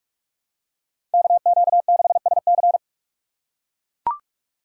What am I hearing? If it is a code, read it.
Morse code, heard as a keyed tone, about 44 wpm, 700 Hz. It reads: KQ6SC